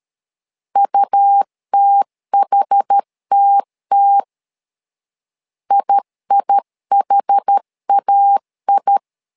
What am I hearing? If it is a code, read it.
Morse code, heard as a keyed tone, about 13 wpm, 800 Hz. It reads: UTHTT IIHAI
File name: LEZIONI / 2 UHDN / 2 UH 12wpm 1439